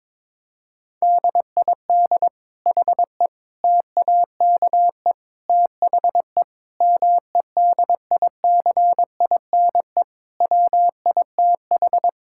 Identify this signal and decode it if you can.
Morse code, heard as a keyed tone, about 22 wpm, 700 Hz. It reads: DID HE TAKE THE MEDICINE WITH
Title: Sentences from Top 1000 Words - Review Part 6 - 22wpm